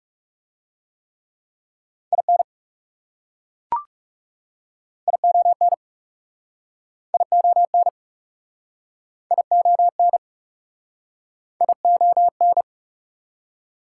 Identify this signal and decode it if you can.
Morse code, heard as a keyed tone, about 40 wpm, 700 Hz. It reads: IN ION ION ION ION